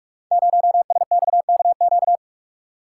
Morse code, heard as a keyed tone, about 44 wpm, 700 Hz. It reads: 0SXKQ
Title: Code Groups - 5 Letter:Numbers - 45wpm